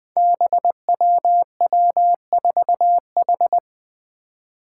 Morse code, heard as a keyed tone, about 21 wpm, 700 Hz. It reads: BWW4H
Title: Five Challenging Letter-Number Combinations 20wpm